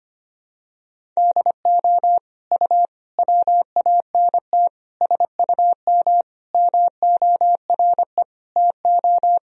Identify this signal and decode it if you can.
Morse code, heard as a keyed tone, about 25 wpm, 700 Hz. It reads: DO U WANT SUM MORE TO
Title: Sentences from Top 300 Words and Abbreviations 25wpm